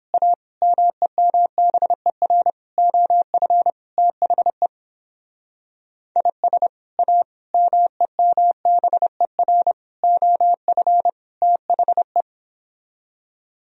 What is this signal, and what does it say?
Morse code, heard as a keyed tone, about 29 wpm, 700 Hz. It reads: A MEMBER OF THE IS A MEMBER OF THE